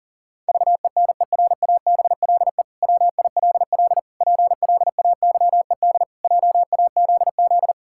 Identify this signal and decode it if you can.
Morse code, heard as a keyed tone, about 40 wpm, 700 Hz. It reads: VENERABLE WILL PLAYED JAZZ
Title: Pangram Challenge v2 - 40wpm